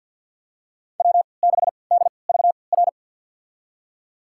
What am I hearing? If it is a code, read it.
Morse code, heard as a keyed tone, about 53 wpm, spaced as for 29 wpm, 700 Hz. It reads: W6DVR
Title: USA Call Signs - 5 Characters - 3x Character Spacing - 50wpm